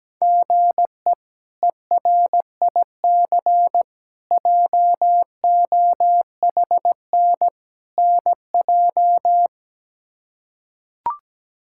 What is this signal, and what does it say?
Morse code, heard as a keyed tone, about 17 wpm, 700 Hz. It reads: GE ERIC JOHN NJ
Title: K1USN SST QSO Practice - 17wpm